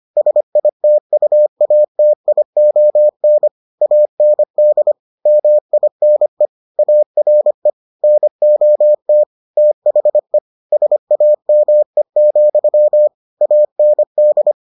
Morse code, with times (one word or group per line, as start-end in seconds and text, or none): 0.2-3.5 SITUATION
3.8-4.9 AND
5.3-6.5 MINE
6.8-7.7 ARE
8.0-9.2 NOT
9.6-10.4 THE
10.7-13.1 SAME,
13.4-14.5 AND